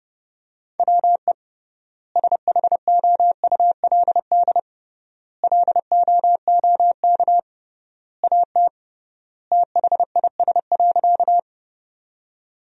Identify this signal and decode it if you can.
Morse code, heard as a keyed tone, about 32 wpm, 700 Hz. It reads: WE SHOULD LOOK AT THIS.
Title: 3x Word-Spacing - Sentences from Top 300 Words - 30wpm